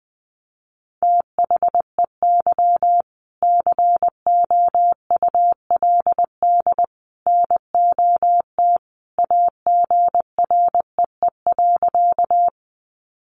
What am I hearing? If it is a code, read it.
Morse code, heard as a keyed tone, about 19 wpm, 700 Hz. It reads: THEY COULD NOT AGREE.
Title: Sentences from Top 1100 Words 20wpm